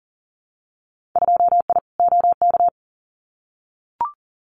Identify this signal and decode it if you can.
Morse code, heard as a keyed tone, about 40 wpm, 700 Hz. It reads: 2I OK
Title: Winter Field Day Exchanges Encore 40wpm